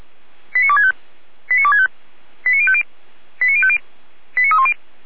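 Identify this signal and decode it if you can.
Five-tone selective call (ZVEI), heard as quick runs of five tones, tones of about 70 ms. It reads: 89265 89265 89050 89050 89310
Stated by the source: ZVEI five-tone selective call 89170 received on an SDR.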